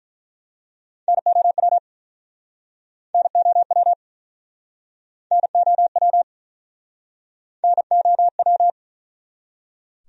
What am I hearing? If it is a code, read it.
Morse code, heard as a keyed tone, about 50 wpm, 700 Hz. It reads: NOW NOW NOW NOW